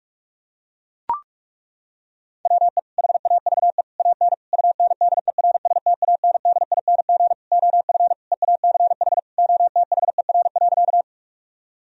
Morse code, heard as a keyed tone, about 42 wpm, 700 Hz. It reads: WE HAVE AN UNDERSTANDING OF EACH OTHER.
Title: Sentences from Top 1000 Words - Review Part 3 - 45wpm